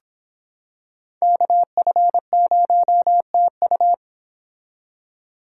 Morse code, heard as a keyed tone, about 26 wpm, 700 Hz. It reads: KF0TU